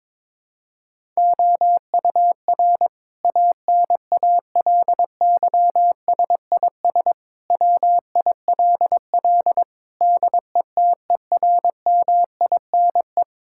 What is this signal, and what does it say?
Morse code, heard as a keyed tone, about 22 wpm, 700 Hz. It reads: OUR ANALYSIS WILL DETERMINE
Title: Sentences from Top 1000 Words - Review Part 5 - 22wpm